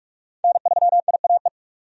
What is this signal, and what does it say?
Morse code, heard as a keyed tone, about 46 wpm, 700 Hz. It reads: N3IAE
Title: USA Call Signs - 5 Characters - 45wpm